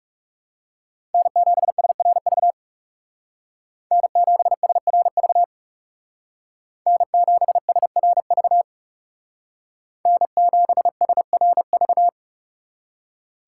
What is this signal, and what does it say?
Morse code, heard as a keyed tone, about 38 wpm, 700 Hz. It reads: N7SRV N7SRV N7SRV N7SRV